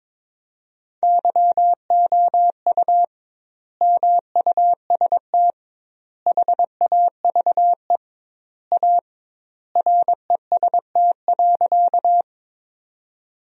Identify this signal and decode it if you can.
Morse code, heard as a keyed tone, about 21 wpm, 700 Hz. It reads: YOU MUST HAVE A REST.
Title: 2x Word-Spacing - Sentences from Top 500 Words - 22wpm